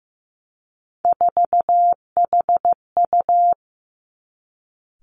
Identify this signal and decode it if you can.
Morse code, heard as a keyed tone, about 15 wpm, 700 Hz. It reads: 4HU
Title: Three Challenging Letter-Number Combinations 15wpm